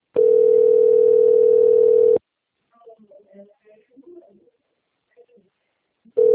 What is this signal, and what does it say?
Ringback tone (North America), 440 + 480 Hz.